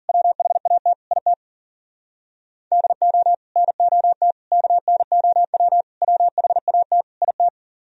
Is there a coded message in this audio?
Yes: Morse code from 0.1 to 7.5 s, heard as a keyed tone, about 52 wpm, 700 Hz. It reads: WHAT IT DO NOT KNOW WHAT IT